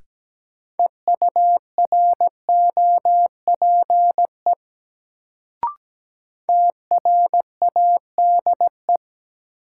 Morse code, heard as a keyed tone, about 17 wpm, 700 Hz. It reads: EUROPE TRADE
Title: Sets of 5 Words Encore - New Word Drill - 17wpm